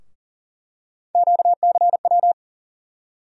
Morse code, heard as a keyed tone, about 40 wpm, 700 Hz. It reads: QCW
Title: USA Call Sign Postfixes - 3 Letters 40wpm